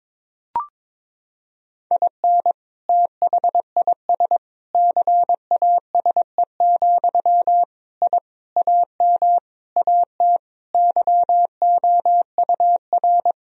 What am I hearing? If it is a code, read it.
Morse code, heard as a keyed tone, about 22 wpm, 700 Hz. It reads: IN THIS CASE, I AM AT YOUR
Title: Sentences from Top 1000 Words - Review Part 4 - 22wpm